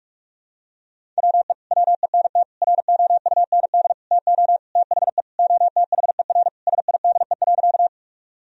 Morse code, heard as a keyed tone, about 44 wpm, 700 Hz. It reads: WE WENT ROUND TO THE OTHER SIDE.
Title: Sentences from Top 1000 Words - Review Part 1 - 45wpm